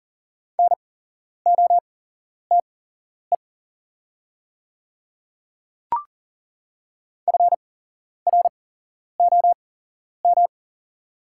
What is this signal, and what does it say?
Morse code, heard as a keyed tone, about 37 wpm, spaced as for 11 wpm, 700 Hz. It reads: NOTE FROM
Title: 8x Character Spacing - Top 500 Words - 40wpm